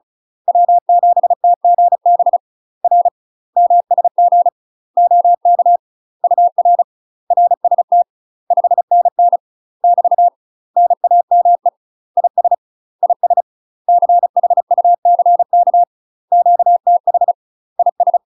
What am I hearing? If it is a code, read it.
Morse code, heard as a keyed tone, about 34 wpm, 700 Hz. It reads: W7TGB R MSG OK UR RST 5NN <BT> NAME IS IS CHUCK QTH IS
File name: UK MASTER QSOs TRAINING / 35WPM / UK MASTER QSOs 0034